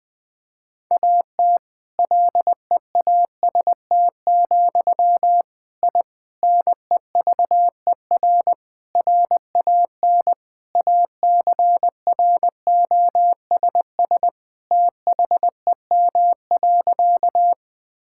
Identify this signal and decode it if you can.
Morse code, heard as a keyed tone, about 20 wpm, 700 Hz. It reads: AT LEAST, I NEVER RAN ACROSS THEM.
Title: Sentences from Top 1000 Words - Review Part 5 - 20wpm